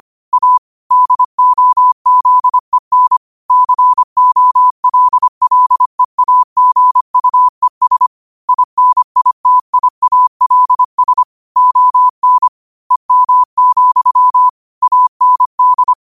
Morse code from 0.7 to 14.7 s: DOZEN COLLEAGUES INITIALS ON EM,

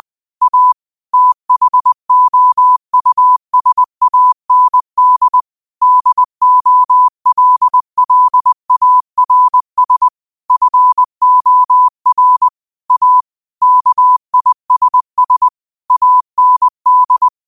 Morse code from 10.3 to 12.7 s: FOR

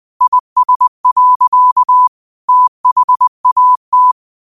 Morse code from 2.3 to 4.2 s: THAT

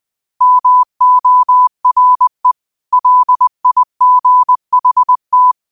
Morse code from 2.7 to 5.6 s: LIGHT

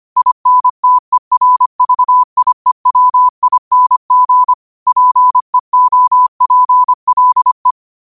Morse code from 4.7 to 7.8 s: PEOPLE

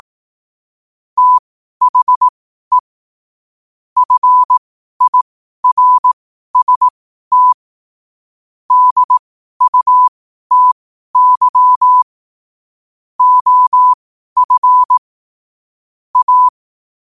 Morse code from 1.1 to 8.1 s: THE FIRST

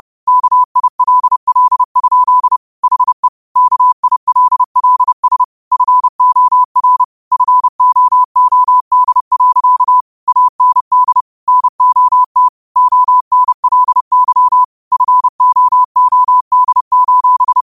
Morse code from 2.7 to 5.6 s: HE KILLS

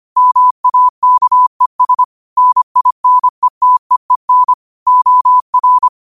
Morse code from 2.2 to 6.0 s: NINETEEN OR

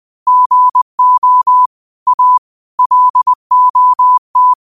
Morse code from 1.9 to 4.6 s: A LOT

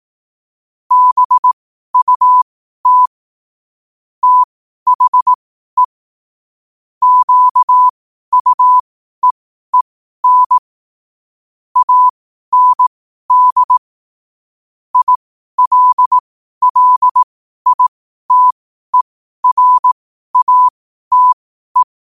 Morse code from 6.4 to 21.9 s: QUEEN AND ILLITERATE

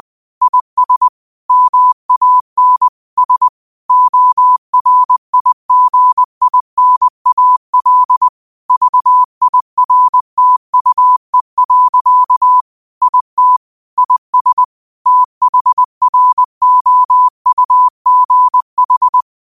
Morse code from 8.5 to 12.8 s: VIRTUE.

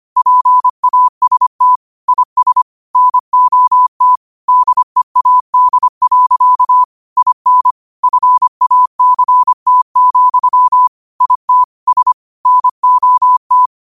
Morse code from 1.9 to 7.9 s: IS NOT DEAD. IN

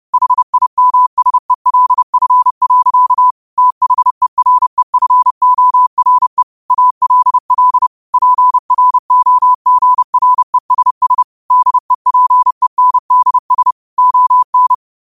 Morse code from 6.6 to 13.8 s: ALL PROGRESS DEPENDS

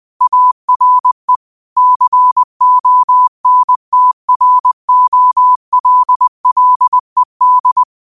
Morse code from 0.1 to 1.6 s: ARE

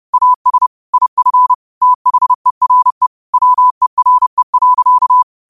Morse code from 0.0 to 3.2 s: AS IF THERE

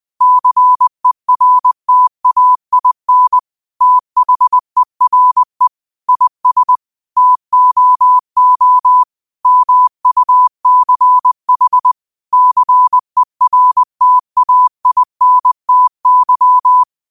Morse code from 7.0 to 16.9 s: TOO MUCH CERTAINTY

